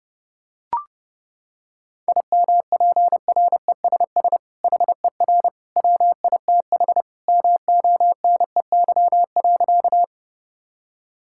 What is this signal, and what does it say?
Morse code, heard as a keyed tone, about 32 wpm, 700 Hz. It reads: IMPRESS HER WITH MONEY.
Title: Sentences from Top 1600 Words 30wpm